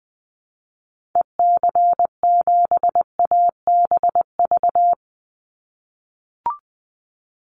Morse code, heard as a keyed tone, about 20 wpm, 700 Hz. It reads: EC7ABV
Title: Call Signs - 20wpm